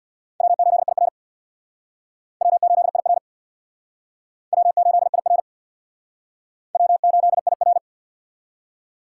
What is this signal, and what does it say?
Morse code, heard as a keyed tone, about 63 wpm, 700 Hz. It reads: W8IR W8IR W8IR W8IR